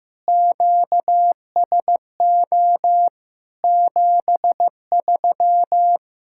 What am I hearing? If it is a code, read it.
Morse code, heard as a keyed tone, about 16 wpm, 700 Hz. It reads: QSO 73
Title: K1USN SST QSO Practice 15wpm